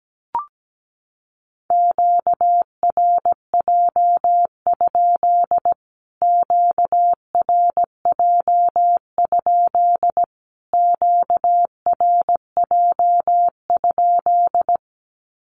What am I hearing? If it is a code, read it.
Morse code, heard as a keyed tone, about 17 wpm, 700 Hz. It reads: QRJ? QRJ? QRJ?
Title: Q-Codes Repeated 3x 17wpm